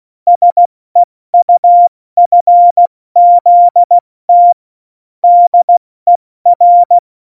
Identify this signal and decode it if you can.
Morse code, heard as a keyed tone, about 16 wpm, spaced as for 14 wpm, 700 Hz. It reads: SEUFZT DER
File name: Morsekurs / Witze / Witz 26 Mai, 2020 70 F80